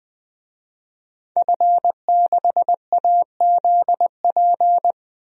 Morse code, heard as a keyed tone, about 20 wpm, 700 Hz. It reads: F6AZP
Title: DXCC-100 Call Signs - 5 Characters - Repeated 2x - 20wpm